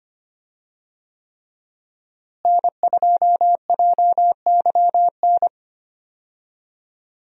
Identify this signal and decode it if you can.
Morse code, heard as a keyed tone, about 25 wpm, 700 Hz. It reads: N2JYN
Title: USA Call Signs - 5 Characters - Repeated 3x - 25wpm